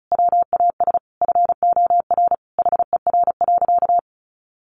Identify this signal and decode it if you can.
Morse code, heard as a keyed tone, about 34 wpm, 700 Hz. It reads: WAS FOR HER.
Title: Sentences from Top 400 Words 35wpm